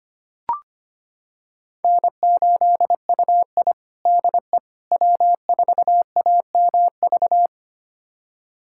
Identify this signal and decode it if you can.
Morse code, heard as a keyed tone, about 25 wpm, 700 Hz. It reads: N8UI DE W4AMV
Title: USA Callsign DE Callsign 25wpm